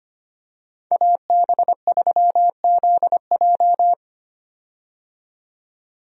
Morse code, heard as a keyed tone, about 25 wpm, 700 Hz. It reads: AB3ZJ
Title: USA Call Signs - 5 Characters - 25wpm